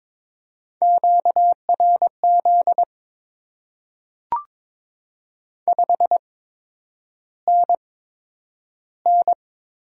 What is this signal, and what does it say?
Morse code, heard as a keyed tone, about 22 wpm, spaced as for 9 wpm, 700 Hz. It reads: QRZ 5NN
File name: Top 90 QSO Elements - 8x Character Spacing - 22wpm